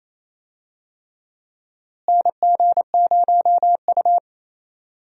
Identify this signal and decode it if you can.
Morse code, heard as a keyed tone, about 28 wpm, 700 Hz. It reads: NG0U
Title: USA Call Signs - 4 Characters - Repeated 3x - 28wpm